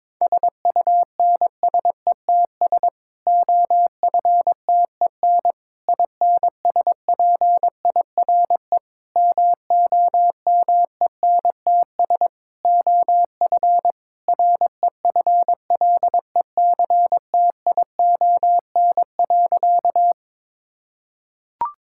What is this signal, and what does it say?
Morse code, heard as a keyed tone, about 22 wpm, 700 Hz. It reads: SUNSETS OFTEN INSPIRE MOMENTS OF REFLECTION.
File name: Difficult Sentences with Hints - 22wpm